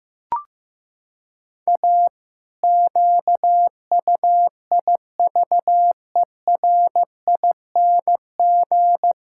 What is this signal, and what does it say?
Morse code, heard as a keyed tone, about 15 wpm, 700 Hz. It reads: A QUIVERING
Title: Pangram Challenge v2 - 15wpm